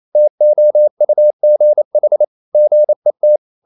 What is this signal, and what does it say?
Morse code, heard as a keyed tone, about 28 wpm, 600 Hz. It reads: TOUGH GET
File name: CW / Practice / EnglishSayings28